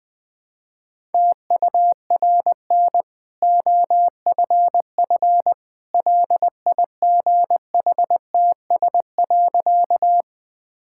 Morse code, heard as a keyed tone, about 20 wpm, 700 Hz. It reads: TURN OFF LIGHTS.
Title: Easy Sentences with Hints - 20wpm